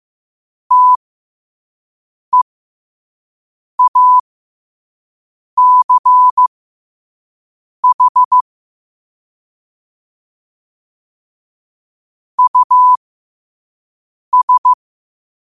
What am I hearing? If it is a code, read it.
Morse code, heard as a keyed tone, about 15 wpm, spaced as for 5 wpm, 1000 Hz. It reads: TEACH US
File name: qotdfiles / 5wpm